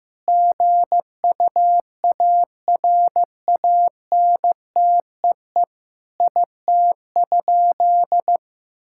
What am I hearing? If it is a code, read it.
Morse code, heard as a keyed tone, about 15 wpm, 700 Hz. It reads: GUARANTEE IT?
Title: Sentences from Top 1600 Words 15wpm